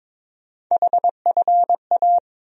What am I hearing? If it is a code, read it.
Morse code, heard as a keyed tone, about 21 wpm, 700 Hz. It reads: HFA
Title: USA Call Sign Postfixes - 3 Letters 22wpm